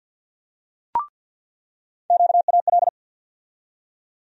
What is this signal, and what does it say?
Morse code, heard as a keyed tone, about 48 wpm, 700 Hz. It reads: QAL
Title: USA Call Sign Postfixes - 3 Letters 50wpm